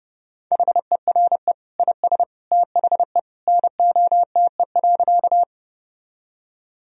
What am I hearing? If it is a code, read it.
Morse code, heard as a keyed tone, about 28 wpm, 700 Hz. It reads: HERE IS THE NOTE.
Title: Sentences from Top 1000 Words - Review Part 2 - 30wpm